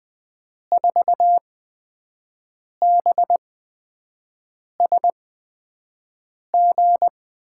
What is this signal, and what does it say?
Morse code, heard as a keyed tone, about 21 wpm, spaced as for 5 wpm, 700 Hz. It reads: 4BSG
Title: USA Call Signs - 6 Characters - 8x Character Spacing - 20wpm